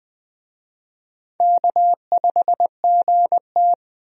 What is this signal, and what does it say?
Morse code, heard as a keyed tone, about 20 wpm, 700 Hz. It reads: K5GT
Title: USA Call Signs - 4 Characters - 20wpm